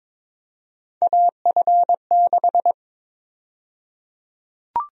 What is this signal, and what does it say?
Morse code, heard as a keyed tone, about 22 wpm, 700 Hz. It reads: AF6
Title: USA Call Sign Prefixes - 22wpm